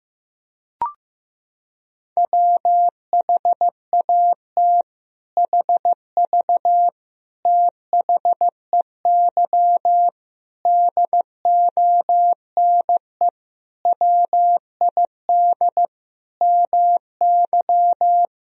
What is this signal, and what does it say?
Morse code, heard as a keyed tone, about 15 wpm, 700 Hz. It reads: WHAT HV THEY DONE WID MY
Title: Sentences from Top 500 Words and Abbreviations 15wpm